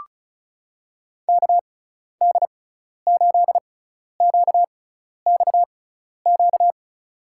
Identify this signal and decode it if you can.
Morse code, heard as a keyed tone, about 35 wpm, spaced as for 12 wpm, 690 Hz. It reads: KD8QXQ
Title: USA Call Signs - 6 Characters - 6x Character Spacing - 35wpm